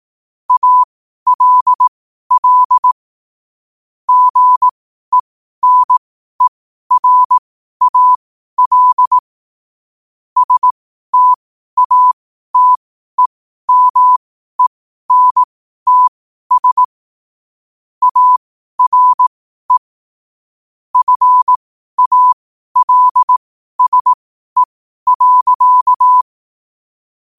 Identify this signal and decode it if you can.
Morse code, heard as a keyed tone, about 18 wpm, spaced as for 12 wpm, 1000 Hz. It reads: ALL GENERAL STATEMENTS ARE FALSE.
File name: qotdfiles / 12wpm